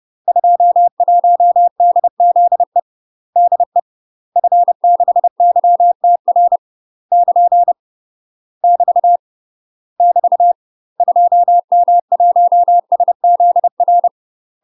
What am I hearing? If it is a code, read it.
Morse code, heard as a keyed tone, about 30 wpm, 700 Hz. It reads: 21DZE DE F6YTR <KN> <BT> <BT> 2M1SZR
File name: UK MASTER QSOs TRAINING / 30WPM / UK MASTER QSOs 0003